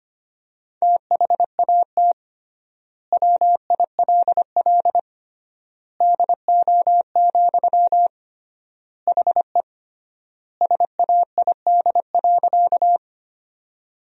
Morse code, heard as a keyed tone, about 25 wpm, 700 Hz. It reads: THAT WILL DO, HE SAID.
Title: 3x Word-Spacing - Sentences from Top 100 Words - 25wpm